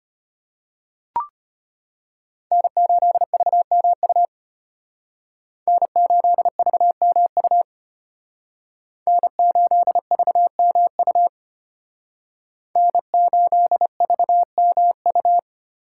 Morse code, heard as a keyed tone, about 33 wpm, 700 Hz. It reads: N8VMU N8VMU N8VMU N8VMU